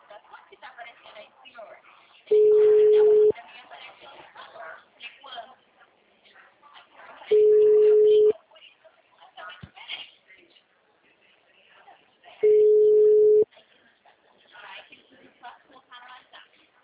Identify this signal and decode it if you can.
Ringback tone (Europe, 425 Hz), with speech.
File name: out- -2008- - - .83154